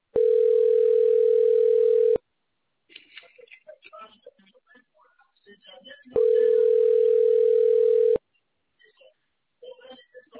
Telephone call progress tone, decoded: ringback tone